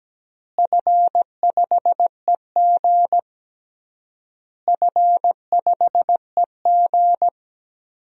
Morse code, heard as a keyed tone, about 17 wpm, 700 Hz. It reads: F5EG F5EG
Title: DXCC-100 Call Signs - 4 Characters - Repeated 3x - 17wpm